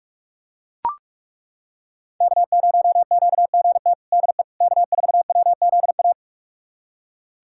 Morse code, heard as a keyed tone, about 42 wpm, 700 Hz. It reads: K0QGT DE K4WZA
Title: USA Callsign DE Callsign - 5 Characters - 45wpm